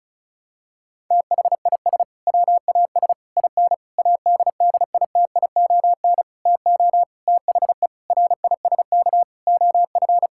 Morse code, heard as a keyed tone, about 36 wpm, 700 Hz. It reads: THIS WAS IN ADDITION TO THE RISK OF
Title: Sentences from Top 1000 Words - Review Part 6 - 35wpm